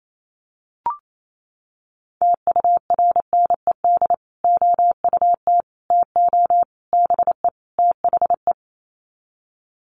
Morse code, heard as a keyed tone, about 28 wpm, 700 Hz. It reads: TURNED OUT TO BE THE
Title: Sets of 5 Words Encore 28wpm